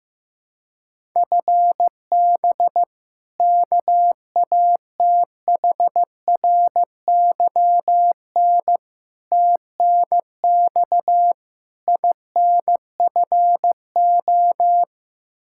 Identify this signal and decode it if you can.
Morse code, heard as a keyed tone, about 15 wpm, 700 Hz. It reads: FB KATHRYN TNX INFO
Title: Practice QSO - Exchange 3 - More Info - 15wpm